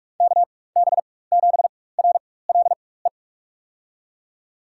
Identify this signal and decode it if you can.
Morse code, heard as a keyed tone, about 45 wpm, spaced as for 21 wpm, 700 Hz. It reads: KB7RLE
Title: USA Call Signs - 6 Characters - 4x Character Spacing - 45wpm